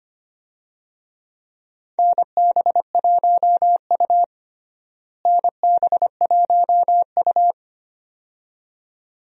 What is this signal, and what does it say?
Morse code, heard as a keyed tone, about 25 wpm, 700 Hz. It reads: NB1U NB1U
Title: USA Call Signs - 4 Characters - Repeated 2x - 25wpm